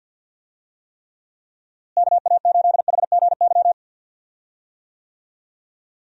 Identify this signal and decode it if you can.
Morse code, heard as a keyed tone, about 52 wpm, 700 Hz. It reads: KA8HGY
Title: USA Call Signs - 6 Characters - Repeated 2x - 50wpm